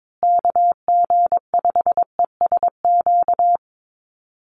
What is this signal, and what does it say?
Morse code, heard as a keyed tone, about 22 wpm, 700 Hz. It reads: KG5ESQ
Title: USA Call Signs Repeated 3x 22wpm